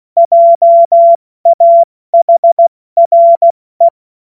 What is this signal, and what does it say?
Morse code, heard as a keyed tone, about 16 wpm, spaced as for 14 wpm, 680 Hz. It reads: JAHRE
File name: Morsekurs / Witze / Witz 19. April 2022 70 F80